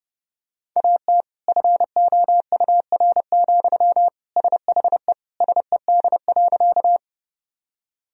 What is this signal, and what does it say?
Morse code, heard as a keyed tone, about 32 wpm, 700 Hz. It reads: AT FOUR, SHE SED.
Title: Sentences from Top 300 Words and Abbreviations 30wpm